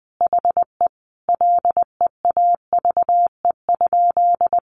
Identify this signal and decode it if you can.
Morse code, heard as a keyed tone, about 20 wpm, 700 Hz. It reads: HE LEAVE?
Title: Sentences from Top 400 Words 20wpm